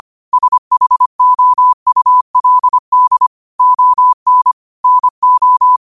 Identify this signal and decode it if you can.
Morse code, heard as a keyed tone, about 25 wpm, 1000 Hz. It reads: SHOULD ON NO
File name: qotdfiles / 25wpm